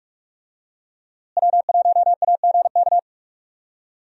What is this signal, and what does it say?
Morse code, heard as a keyed tone, about 46 wpm, 700 Hz. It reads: W1AGK
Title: USA Call Signs - 5 Characters - Repeated 2x - 45wpm